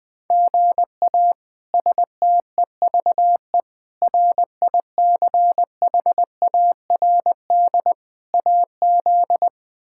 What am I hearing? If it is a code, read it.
Morse code, heard as a keyed tone, about 20 wpm, 700 Hz. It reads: GA STEVE RICHARD AZ
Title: K1USN SST QSO Practice - 20wpm